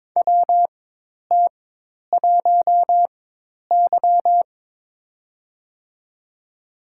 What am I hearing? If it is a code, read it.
Morse code, heard as a keyed tone, about 21 wpm, spaced as for 10 wpm, 700 Hz. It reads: WT1Y
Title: USA Call Signs - 4 Characters - 4x Character Spacing - 22wpm